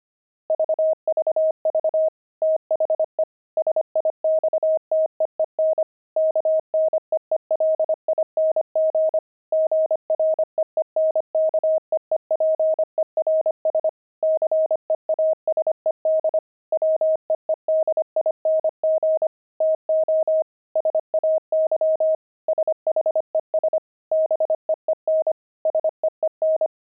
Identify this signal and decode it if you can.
Morse code, heard as a keyed tone, about 25 wpm, 630 Hz. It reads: VVV THE SIXTEEN KNEELING GREENKEEPERS CEASED WEEDING TO SAY SHES BEEN SEEN
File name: fs-25-cs-25